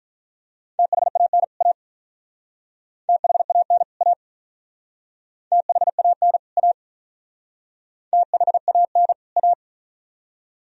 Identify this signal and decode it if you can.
Morse code, heard as a keyed tone, about 44 wpm, 700 Hz. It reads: THAN A THAN A THAN A THAN A